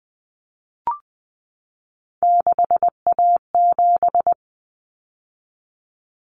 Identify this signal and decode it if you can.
Morse code, heard as a keyed tone, about 20 wpm, 700 Hz. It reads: 6A7
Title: Common Cut Numbers - 3-Digit 20wpm